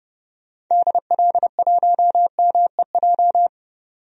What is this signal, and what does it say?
Morse code, heard as a keyed tone, about 29 wpm, 700 Hz. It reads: DL1MEJ
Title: DXCC-100 Call Signs - 6 Characters - 30wpm